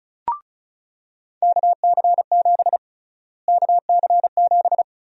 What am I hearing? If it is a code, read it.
Morse code, heard as a keyed tone, about 34 wpm, 700 Hz. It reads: KC7 KC7
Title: USA Call Sign Prefixes - Repeated 2x - 35wpm